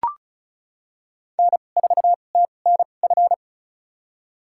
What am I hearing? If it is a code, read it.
Morse code, heard as a keyed tone, about 36 wpm, spaced as for 25 wpm, 700 Hz. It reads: N4TNF
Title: USA Call Signs - 5 Characters - 2x Character Spacing - 35wpm